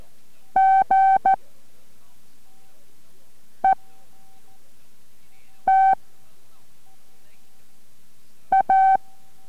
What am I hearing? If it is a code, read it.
Morse code, heard as a keyed tone, about 14 wpm, spaced as for 4 wpm, 770 Hz. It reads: GET A